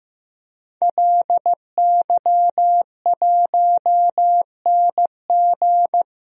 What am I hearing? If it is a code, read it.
Morse code, heard as a keyed tone, about 15 wpm, 700 Hz. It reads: LY1NG
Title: DXCC-100 Call Signs - 5 Characters - Repeated 3x - 15wpm